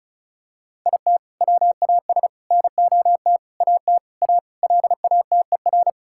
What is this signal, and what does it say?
Morse code, heard as a keyed tone, about 36 wpm, 700 Hz. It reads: IT WAS NOT AT A LATER
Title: Sentences from Top 1000 Words - Review Part 5 - 35wpm